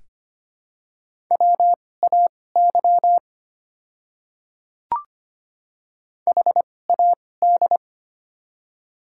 Morse code, heard as a keyed tone, about 25 wpm, spaced as for 18 wpm, 700 Hz. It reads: WAY HAD
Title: 2x Character Spacing - Top 100 Words - 25wpm